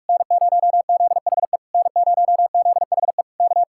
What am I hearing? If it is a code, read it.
Morse code, heard as a keyed tone, about 59 wpm, 700 Hz. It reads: N0ZHE N0ZHE K